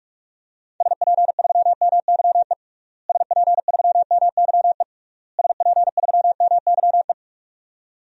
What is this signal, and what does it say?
Morse code, heard as a keyed tone, about 46 wpm, 700 Hz. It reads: SP3MYE SP3MYE SP3MYE
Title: DXCC-100 Call Signs - 6 Characters - Repeated 3x - 45wpm